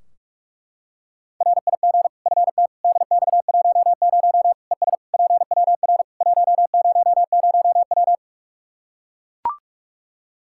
Morse code, heard as a keyed tone, about 42 wpm, 700 Hz. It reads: RIG FT DX10 ES PWR 100W
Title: K1USN SST QSO Practice 45wpm